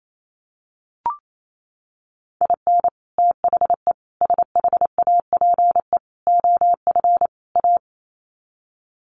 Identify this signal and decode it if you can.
Morse code, heard as a keyed tone, about 28 wpm, 700 Hz. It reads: IN THE SHAPE OF A
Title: Sets of 5 Words Encore 28wpm